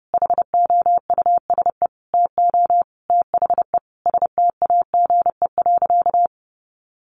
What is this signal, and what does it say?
Morse code, heard as a keyed tone, about 28 wpm, 700 Hz. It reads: HOUSE TO THE STAGE.
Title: Sentences from Top 1200 Words 30wpm